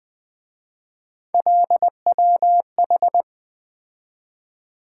Morse code, heard as a keyed tone, about 24 wpm, 700 Hz. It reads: LWH
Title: Speed-Racing - Three Letter-Number Combinations - 20wpm